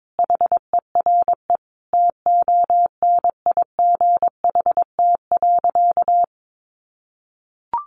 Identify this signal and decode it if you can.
Morse code, heard as a keyed tone, about 22 wpm, 700 Hz. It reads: HERE TONIGHT.
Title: Sentences from Top 600 Words 22wpm